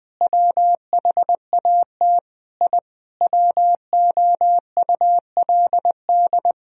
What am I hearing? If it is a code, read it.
Morse code, heard as a keyed tone, about 20 wpm, 700 Hz. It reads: WHAT I WOULD
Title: Sentences from Top 1000 Words - Review Part 3 - 20wpm